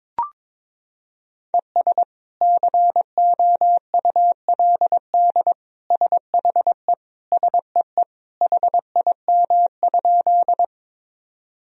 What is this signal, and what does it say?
Morse code, heard as a keyed tone, about 22 wpm, 700 Hz. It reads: ES COULD SHE SEE HIM?
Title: Sentences from Top 100 Words and Abbreviations 22wpm